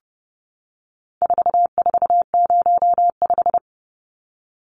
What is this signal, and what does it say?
Morse code, heard as a keyed tone, about 30 wpm, 700 Hz. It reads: 4405
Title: Numbers - 4-Digit 30wpm